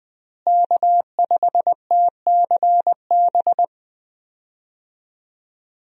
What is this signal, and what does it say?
Morse code, heard as a keyed tone, about 21 wpm, 700 Hz. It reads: K5TCB
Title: USA Call Signs - 5 Characters - Repeated 2x - 20wpm